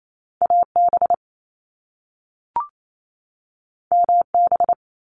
Morse code, heard as a keyed tone, about 28 wpm, 700 Hz. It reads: AB MB
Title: Canadian Provinces and Territories 28wpm